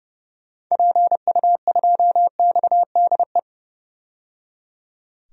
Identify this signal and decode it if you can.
Morse code, heard as a keyed tone, about 30 wpm, 700 Hz. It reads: PU2XDE
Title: DXCC-100 Call Signs - 6 Characters - 30wpm